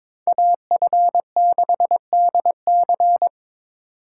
Morse code, heard as a keyed tone, about 26 wpm, 700 Hz. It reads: AF6DC
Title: Speed Racing - USA 2x2 Call Signs 22wpm